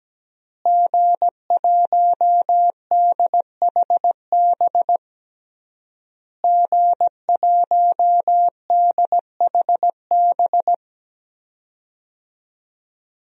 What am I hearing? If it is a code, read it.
Morse code, heard as a keyed tone, about 17 wpm, 700 Hz. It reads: G1DHB G1DHB
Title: DXCC-100 Call Signs - 5 Characters - Repeated 3x - 17wpm